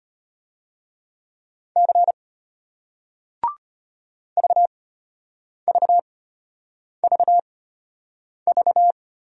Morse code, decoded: C VVVV

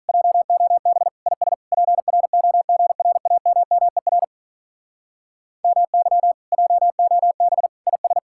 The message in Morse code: JOB IS PROGRAMMER MY JOB IS

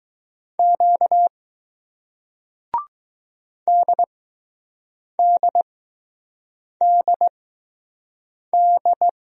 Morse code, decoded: Q DDDD

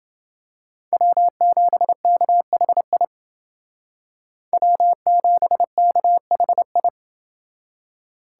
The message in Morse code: W7KHI W7KHI

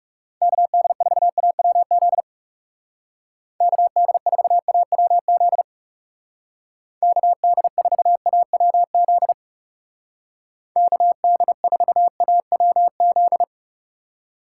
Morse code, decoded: KD4AWZ KD4AWZ KD4AWZ KD4AWZ

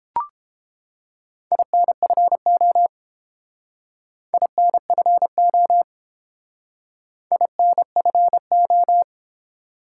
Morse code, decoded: INFO INFO INFO

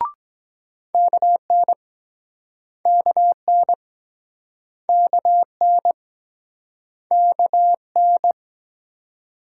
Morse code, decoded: KN KN KN KN

